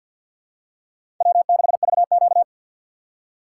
Morse code, heard as a keyed tone, about 52 wpm, 700 Hz. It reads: W6VQ
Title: USA Call Signs - 4 Characters - 2x Character Spacing - 50wpm